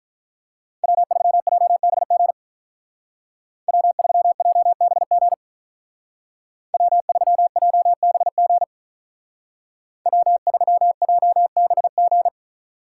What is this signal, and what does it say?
Morse code, heard as a keyed tone, about 44 wpm, 700 Hz. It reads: W3JBG W3JBG W3JBG W3JBG